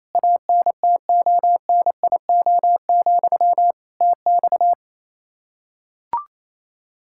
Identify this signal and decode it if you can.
Morse code, heard as a keyed tone, about 28 wpm, 700 Hz. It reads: ANTONIO, TX
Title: K1USN SST QSO Practice 28wpm